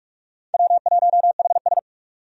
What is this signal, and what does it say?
Morse code, heard as a keyed tone, about 46 wpm, 700 Hz. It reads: W1HS
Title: USA Call Signs - 4 Characters - 45wpm